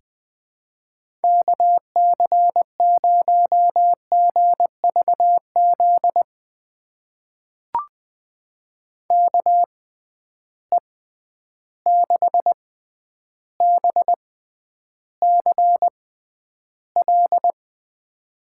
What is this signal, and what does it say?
Morse code, heard as a keyed tone, about 20 wpm, spaced as for 10 wpm, 700 Hz. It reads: KC0GVZ KE6BCL
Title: USA Call Signs - 6 Characters - 6x Character Spacing - 20wpm